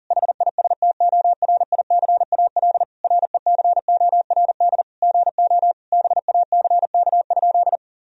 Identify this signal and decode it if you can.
Morse code, heard as a keyed tone, about 40 wpm, 700 Hz. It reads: HISTORICAL RECORD GO BACK?
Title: Sentences from Top 1000 Words - Review Part 4 - 40wpm